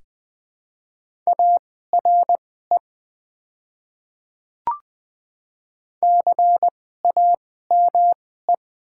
Morse code, heard as a keyed tone, about 20 wpm, spaced as for 15 wpm, 700 Hz. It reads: ARE CAME
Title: 2x Character Spacing - Top 200 Words - 20wpm